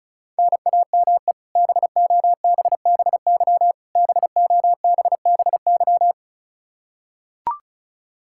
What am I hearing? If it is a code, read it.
Morse code, heard as a keyed tone, about 34 wpm, 700 Hz. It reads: NAME BOBBY BOBBY
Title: Practice QSO - Exchange 1 - Beginning - 35wpm